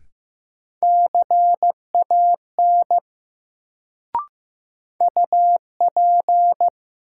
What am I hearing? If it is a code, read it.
Morse code, heard as a keyed tone, about 15 wpm, 700 Hz. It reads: CAN UP
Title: Sentences from Top 100 Words - New Word Drill - 15wpm